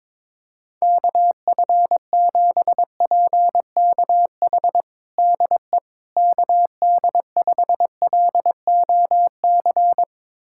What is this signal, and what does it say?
Morse code, heard as a keyed tone, about 22 wpm, 700 Hz. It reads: KF7PKH DE KD5LOC
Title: USA Callsign DE Callsign 22wpm